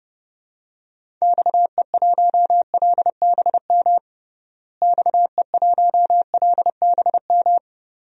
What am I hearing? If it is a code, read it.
Morse code, heard as a keyed tone, about 30 wpm, 700 Hz. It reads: XE1LBM XE1LBM
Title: DXCC-100 Call Signs - 6 Characters - Repeated 2x - 30wpm